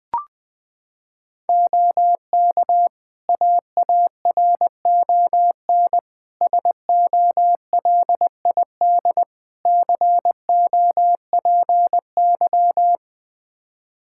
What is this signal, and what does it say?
Morse code, heard as a keyed tone, about 20 wpm, 700 Hz. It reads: OK AARON SOLID COPY
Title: K1USN SST QSO Practice 20wpm